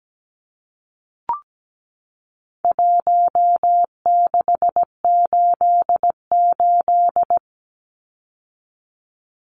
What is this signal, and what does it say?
Morse code, heard as a keyed tone, about 17 wpm, 700 Hz. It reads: 1688
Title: Numbers - Contest Style - 4-Digit 17wpm